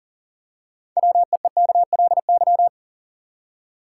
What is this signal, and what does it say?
Morse code, heard as a keyed tone, about 40 wpm, 700 Hz. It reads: WEEKLY
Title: Six Letter Words - 40wpm